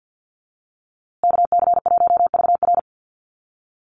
Morse code, heard as a keyed tone, about 50 wpm, 700 Hz. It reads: KC1VR
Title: USA Call Signs Repeated 3x 50wpm